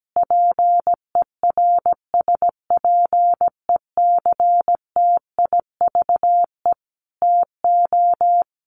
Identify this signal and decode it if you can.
Morse code, heard as a keyed tone, about 17 wpm, 700 Hz. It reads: PERSPECTIVE TO
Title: Sentences from Top 1200 Words 17wpm